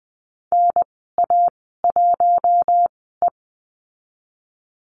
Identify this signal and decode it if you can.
Morse code, heard as a keyed tone, about 20 wpm, spaced as for 14 wpm, 700 Hz. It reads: NA1E
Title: 042 Lesson 8g - Number 1 - Call Signs 20wpm